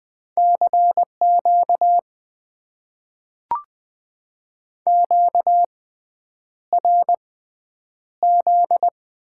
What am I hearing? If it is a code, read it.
Morse code, heard as a keyed tone, about 20 wpm, spaced as for 9 wpm, 700 Hz. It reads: CQ QRZ
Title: Top 90 QSO Elements - 6x Character Spacing - 20wpm